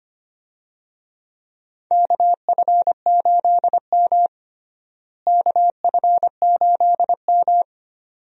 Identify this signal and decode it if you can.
Morse code, heard as a keyed tone, about 25 wpm, 700 Hz. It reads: KF8M KF8M
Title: USA Call Signs - 4 Characters - Repeated 2x - 25wpm